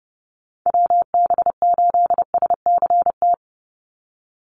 Morse code, heard as a keyed tone, about 30 wpm, 700 Hz. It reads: WB8SCT
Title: Call Signs - 30wpm